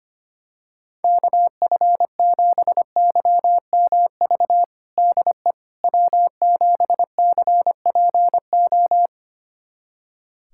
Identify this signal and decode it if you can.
Morse code, heard as a keyed tone, about 25 wpm, 700 Hz. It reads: KF7YMV DE W7CPO
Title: USA Callsign DE Callsign 25wpm